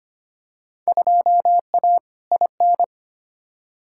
Morse code, heard as a keyed tone, about 25 wpm, 700 Hz. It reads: 2A IN
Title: Field Day Exchanges - 25wpm